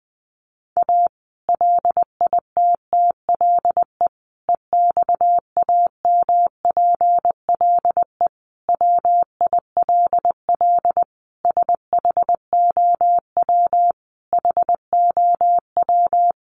Morse code, heard as a keyed tone, about 20 wpm, 700 Hz. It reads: A LITTLE EXAMPLE WILL SHOW HOW
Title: Sentences from Top 1500 Words 20wpm